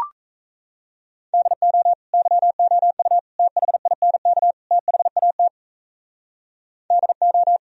Morse code, heard as a keyed tone, about 35 wpm, 700 Hz. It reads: DO YOU THINK THAT DO